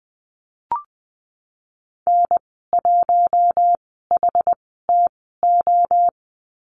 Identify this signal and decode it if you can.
Morse code, heard as a keyed tone, about 19 wpm, spaced as for 14 wpm, 700 Hz. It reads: N1HTO
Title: 069 Lesson 12g - Letter d - Call Signs 20wpm